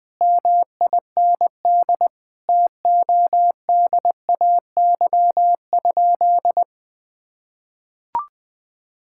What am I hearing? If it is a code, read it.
Morse code, heard as a keyed tone, about 21 wpm, 700 Hz. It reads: MIND TODAY?
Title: Sentences from Top 1000 Words - Review Part 6 - 20wpm